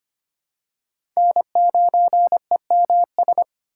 Morse code, heard as a keyed tone, about 25 wpm, 700 Hz. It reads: N9EMS
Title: USA Call Signs - 3x Character Spacing - 25wpm